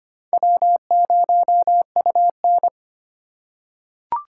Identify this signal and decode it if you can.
Morse code, heard as a keyed tone, about 25 wpm, 700 Hz. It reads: W0UN
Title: USA Call Signs - 4 Characters - 2x Character Spacing - 25wpm